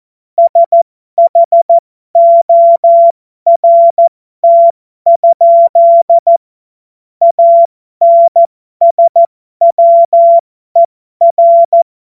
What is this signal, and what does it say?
Morse code, heard as a keyed tone, about 14 wpm, spaced as for 12 wpm, 680 Hz. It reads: SHORT? ANSWER
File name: Morsekurs / Witze